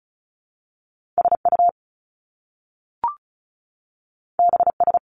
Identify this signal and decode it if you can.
Morse code, heard as a keyed tone, about 36 wpm, 700 Hz. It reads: SU BS
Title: Top Two Letter Combinations in Words 35wpm